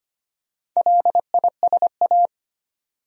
Morse code, heard as a keyed tone, about 25 wpm, 700 Hz. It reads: LISA
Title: USA Common Names - 25wpm